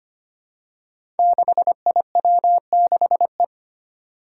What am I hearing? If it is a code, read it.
Morse code, heard as a keyed tone, about 25 wpm, 700 Hz. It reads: 6IW6E
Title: Five Challenging Letter-Number Combinations 25wpm